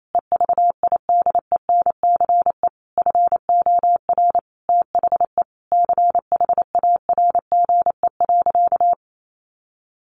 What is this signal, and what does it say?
Morse code, heard as a keyed tone, about 28 wpm, 700 Hz. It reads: EVIDENCE FOR THE CHARGE.
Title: Sentences from Top 1300 Words 28wpm